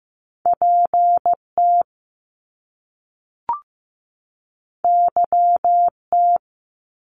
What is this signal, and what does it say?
Morse code, heard as a keyed tone, about 15 wpm, 700 Hz. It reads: PT YT
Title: Top Two Letter Combinations in Words 15wpm